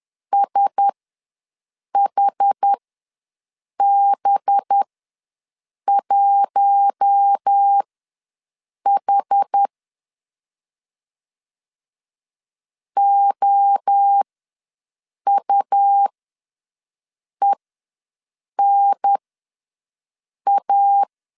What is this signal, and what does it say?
Morse code, heard as a keyed tone, about 10 wpm, spaced as for 6 wpm, 800 Hz. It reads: SHB1H OUENA
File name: LEZIONI / 5 LBVC / 5 LBVC ws 10wpm 3733